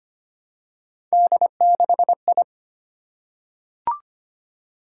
Morse code, heard as a keyed tone, about 25 wpm, 700 Hz. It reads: D6I
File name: Three Letter-Number Combinations - 25wpm